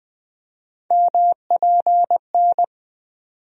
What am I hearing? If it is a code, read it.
Morse code, heard as a keyed tone, about 20 wpm, 700 Hz. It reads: MPN
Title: USA Call Sign Postfixes - 3 Letters 20wpm